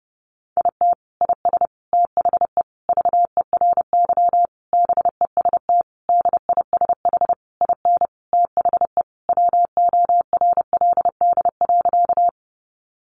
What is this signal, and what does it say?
Morse code, heard as a keyed tone, about 30 wpm, 700 Hz. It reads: IT IS THE VERY BEST DISH IN THE WORLD.
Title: Sentences from Top 1300 Words 30wpm